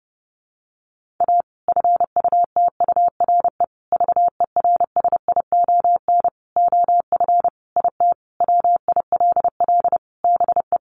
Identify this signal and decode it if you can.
Morse code, heard as a keyed tone, about 30 wpm, 700 Hz. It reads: A FUTURE VERSION OF IT WILL BE
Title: Sentences from Top 1400 Words 30wpm